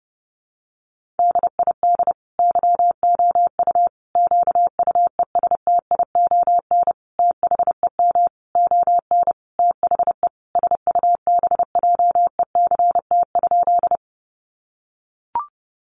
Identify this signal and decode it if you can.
Morse code, heard as a keyed tone, about 30 wpm, 700 Hz. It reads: DID YOU QUESTION THEM ON THE SUBJECT?
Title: Sentences from Top 800 Words 30wpm